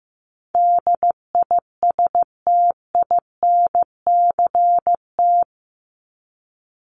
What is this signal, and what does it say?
Morse code, heard as a keyed tone, about 15 wpm, 700 Hz. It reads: DISTINCT
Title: Top 1501-1750 Words 15wpm